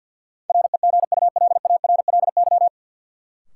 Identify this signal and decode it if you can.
Morse code, heard as a keyed tone, about 53 wpm, 700 Hz. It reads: REGULARLY
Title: Sentences from Top 1400 Words - New Word Drill Alt - 50wpm